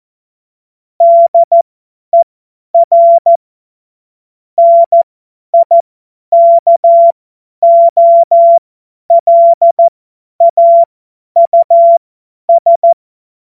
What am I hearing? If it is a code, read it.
Morse code, heard as a keyed tone, about 14 wpm, spaced as for 10 wpm, 680 Hz. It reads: DER NIKOLAUS
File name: Morsekurs / Witze / Witz 15. August 2023 50 F70